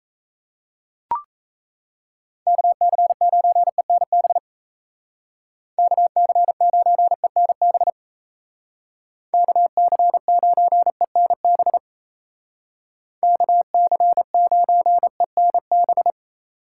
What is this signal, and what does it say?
Morse code, heard as a keyed tone, about 36 wpm, 700 Hz. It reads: KC9ENB KC9ENB KC9ENB KC9ENB